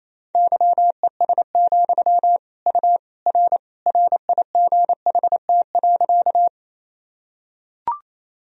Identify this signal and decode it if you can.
Morse code, heard as a keyed tone, about 28 wpm, 700 Hz. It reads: YES, U R RIGHT.
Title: Sentences from Top 500 Words and Abbreviations 28wpm